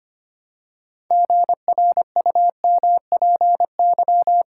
Morse code, heard as a keyed tone, about 25 wpm, 700 Hz. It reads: GRUMPY
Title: Pangram Challenge v2 - 25wpm